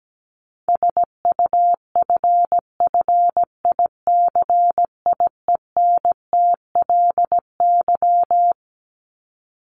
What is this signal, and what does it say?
Morse code, heard as a keyed tone, about 17 wpm, 700 Hz. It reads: SUFFICIENTLY
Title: LY Suffix - Top 100 Words with Base Words 17wpm